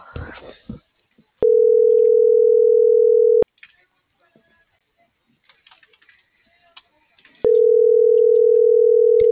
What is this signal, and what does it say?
Ringback tone (North America), 440 + 480 Hz.